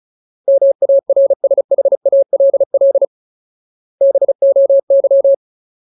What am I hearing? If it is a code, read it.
Morse code, heard as a keyed tone, about 34 wpm, 550 Hz. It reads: MARSHALL BOY
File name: Ultimate Word List / 35WPM Ultimate